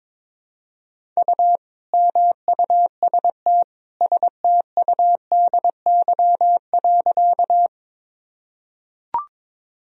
Morse code, heard as a keyed tone, about 22 wpm, 700 Hz. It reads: U MUST STUDY.
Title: Sentences from Top 300 Words and Abbreviations 22wpm